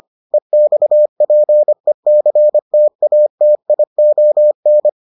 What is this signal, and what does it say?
Morse code, heard as a keyed tone, about 25 wpm, 600 Hz. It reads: EXPECTATION